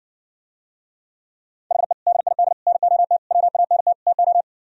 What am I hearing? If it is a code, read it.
Morse code, heard as a keyed tone, about 69 wpm, 700 Hz. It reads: HE DID NOT WANT TO